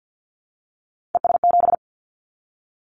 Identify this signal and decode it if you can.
Morse code, heard as a keyed tone, about 50 wpm, 700 Hz. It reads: ES7
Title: Callsign Prefixes for Top 100 DXCC - 50wpm